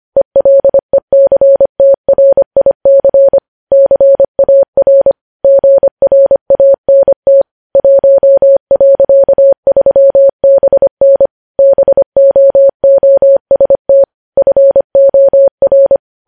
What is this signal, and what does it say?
Morse code, heard as a keyed tone, about 25 wpm, 550 Hz. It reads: ELECTRIC CAR GRANT 1.3BN BOOST FOR
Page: News Headlines in Morse Code at 25 WPM